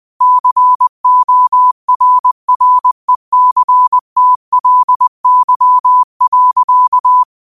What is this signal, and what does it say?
Morse code, heard as a keyed tone, about 20 wpm, 1000 Hz. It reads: CORRECTLY.